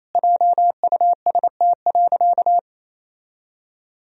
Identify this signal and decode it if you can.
Morse code, heard as a keyed tone, about 28 wpm, 700 Hz. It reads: JUST.
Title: Sentences from Top 1000 Words - Review Part 1 - 28wpm